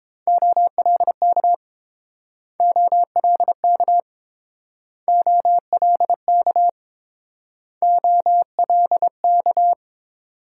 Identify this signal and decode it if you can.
Morse code, heard as a keyed tone, about 28 wpm, 690 Hz. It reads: OLK OLK OLK OLK